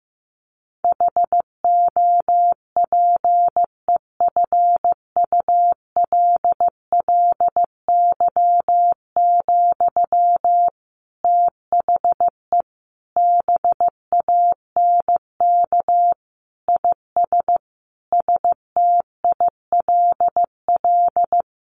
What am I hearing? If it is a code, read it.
Morse code, heard as a keyed tone, about 15 wpm, 700 Hz. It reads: HOPEFULLY, THE BANK IS STILL
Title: Sentences from Top 1500 Words 15wpm